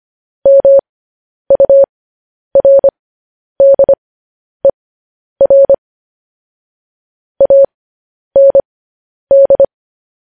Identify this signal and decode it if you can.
Morse code, heard as a keyed tone, about 25 wpm, spaced as for 10 wpm, 550 Hz. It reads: MURDER AND
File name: podcast / News headlines-1x-25x10